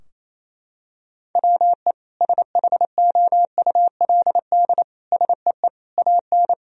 Morse code, heard as a keyed tone, about 28 wpm, 700 Hz. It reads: WE SHOULD SEE AN
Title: Sentences from Top 1000 Words - Review Part 4 - 28wpm